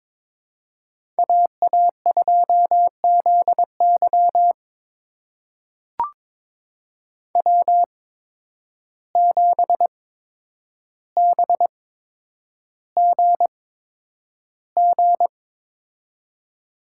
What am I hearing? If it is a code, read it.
Morse code, heard as a keyed tone, about 22 wpm, spaced as for 9 wpm, 700 Hz. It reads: AA2ZY W7BGG
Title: USA Call Signs - 5 Characters - 8x Character Spacing - 22wpm